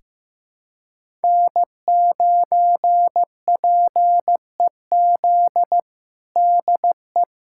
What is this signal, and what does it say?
Morse code, heard as a keyed tone, about 15 wpm, 700 Hz. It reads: N9PEZ DE
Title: USA Callsign DE Callsign - 5 Characters - 15wpm